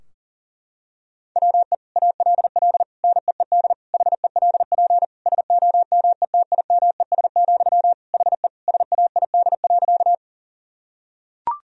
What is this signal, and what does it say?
Morse code, heard as a keyed tone, about 37 wpm, 700 Hz. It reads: WE ALL NEED HELP SOMETIMES, HE SAID.
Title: Sentences from Top 1000 Words - Review Part 3 - 40wpm